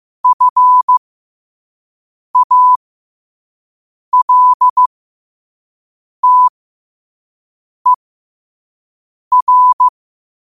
Morse code, heard as a keyed tone, about 15 wpm, spaced as for 5 wpm, 1000 Hz. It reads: FALTER
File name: qotdfiles / 5wpm